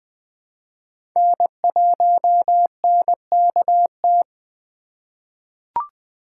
Morse code, heard as a keyed tone, about 20 wpm, 700 Hz. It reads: N1NKT
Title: USA Call Signs - 5 Characters - 3x Character Spacing - 20wpm